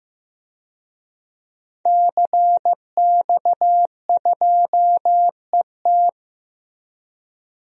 Morse code, heard as a keyed tone, about 15 wpm, 700 Hz. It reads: CX2ET
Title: DXCC-100 Call Signs - 5 Characters - Repeated 3x - 15wpm